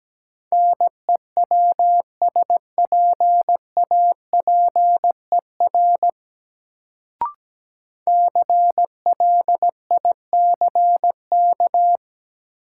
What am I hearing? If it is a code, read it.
Morse code, heard as a keyed tone, about 17 wpm, 700 Hz. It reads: NEWSPAPER CLICK
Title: Sentences from Top 1100 Words - New Word Drill Alt - 17wpm